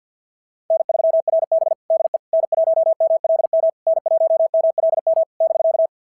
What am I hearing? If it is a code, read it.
Morse code, heard as a keyed tone, about 53 wpm, 690 Hz. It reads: N3RB DE N1MLM N1MLM <BK>
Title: SOTA QSO Practice 50wpm